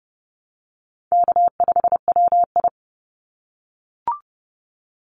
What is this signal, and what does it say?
Morse code, heard as a keyed tone, about 30 wpm, 700 Hz. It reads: K5WI
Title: NA SOTA Chaser Callsigns 30wpm